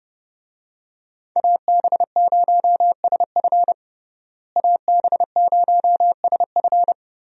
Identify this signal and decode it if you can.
Morse code, heard as a keyed tone, about 30 wpm, 700 Hz. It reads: AB0SF AB0SF
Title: USA Call Signs - 5 Characters - Repeated 3x - 30wpm